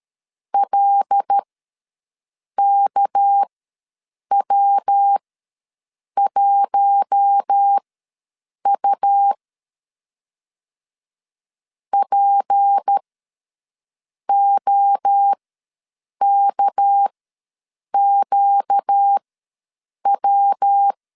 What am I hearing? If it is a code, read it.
Morse code, heard as a keyed tone, about 13 wpm, spaced as for 7 wpm, 800 Hz. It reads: LKW1U POKQW
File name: LEZIONI / 6 PKX0 / 6 PK ws 12wpm 3116